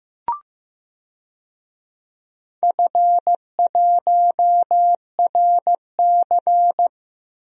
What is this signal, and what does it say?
Morse code, heard as a keyed tone, about 15 wpm, 700 Hz. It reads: F1RC
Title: DXCC-100 Call Signs - 4 Characters - Repeated 3x - 15wpm